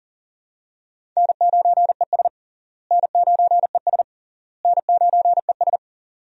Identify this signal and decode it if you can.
Morse code, heard as a keyed tone, about 37 wpm, 700 Hz. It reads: N9ES N9ES N9ES
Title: USA Call Signs - 4 Characters - Repeated 3x - 40wpm